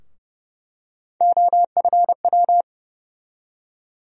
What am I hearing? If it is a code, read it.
Morse code, heard as a keyed tone, about 30 wpm, 700 Hz. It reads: OFW
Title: USA Call Sign Postfixes - 3 Letters 30wpm